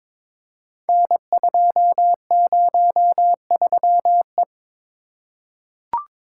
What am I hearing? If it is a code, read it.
Morse code, heard as a keyed tone, about 22 wpm, 700 Hz. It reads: N203E
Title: Code Groups - 5 Letter:Numbers - 22wpm